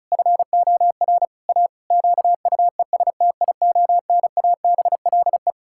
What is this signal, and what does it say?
Morse code, heard as a keyed tone, about 36 wpm, 700 Hz. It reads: FOR A QUESTIONABLE
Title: Pangram Challenge v2 - 35wpm